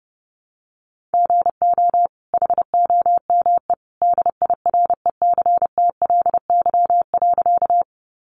Morse code, heard as a keyed tone, about 28 wpm, 700 Hz. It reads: GO HOME DIRECTLY.
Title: Sentences from Top 800 Words 30wpm